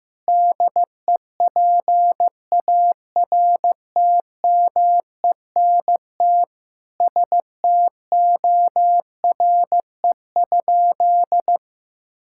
Morse code, heard as a keyed tone, about 15 wpm, 700 Hz. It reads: DEPARTMENT STORE?
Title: Sentences from Top 1000 Words - Review Part 5 - 15wpm